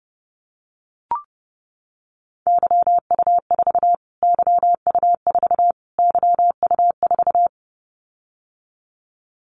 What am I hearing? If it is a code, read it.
Morse code, heard as a keyed tone, about 30 wpm, 700 Hz. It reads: YU4 YU4 YU4
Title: Callsign Prefixes for Top 25 DXCC - Repeated 3x - 30wpm